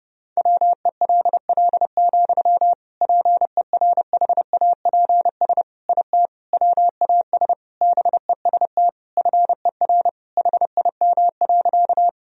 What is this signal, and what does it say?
Morse code, heard as a keyed tone, about 30 wpm, 700 Hz. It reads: WELL, PERHAPS IT WAS BEST FER HIM.
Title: Sentences from Top 500 Words and Abbreviations 30wpm